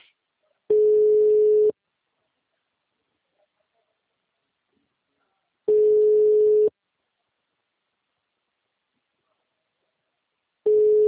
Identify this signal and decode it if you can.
Ringback tone (Europe, 425 Hz).
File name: out- -2017- - - .18